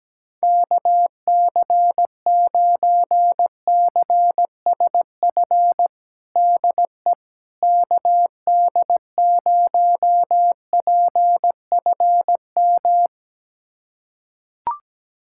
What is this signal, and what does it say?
Morse code, heard as a keyed tone, about 17 wpm, 700 Hz. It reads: KC9CSF DE KD0PFM
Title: USA Callsign DE Callsign - 6 Characters - 17wpm